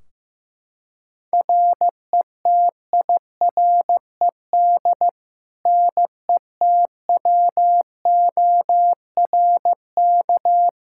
Morse code, heard as a keyed tone, about 15 wpm, 700 Hz. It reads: RETIRED NETWORK
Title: Practice QSO - Exchange 3 - More Info - 15wpm